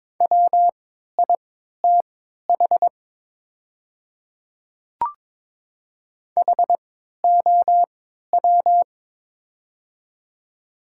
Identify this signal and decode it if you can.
Morse code, heard as a keyed tone, about 22 wpm, spaced as for 13 wpm, 700 Hz. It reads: WITH HOW
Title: 3x Character Spacing - Top 100 Words - 22wpm